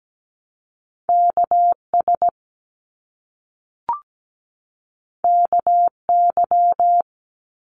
Morse code, heard as a keyed tone, about 17 wpm, 700 Hz. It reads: KS KY
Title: US States 17wpm